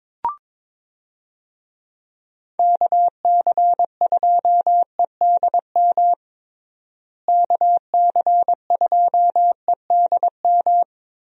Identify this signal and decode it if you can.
Morse code, heard as a keyed tone, about 22 wpm, 700 Hz. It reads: KC2EDM KC2EDM
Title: USA Call Signs - 6 Characters - Repeated 2x - 22wpm